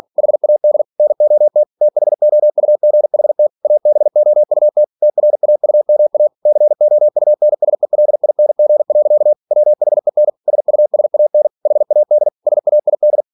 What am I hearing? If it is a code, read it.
Morse code, heard as a keyed tone, about 46 wpm, 600 Hz. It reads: HAD NOT THOUGHT ABOUT TRAUMA COUNSELING. WHEN SUSAN HAD SAID